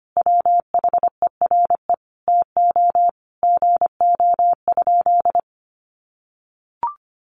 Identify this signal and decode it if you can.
Morse code, heard as a keyed tone, about 25 wpm, 700 Hz. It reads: WHERE TO GO?
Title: Sentences from Top 1300 Words 25wpm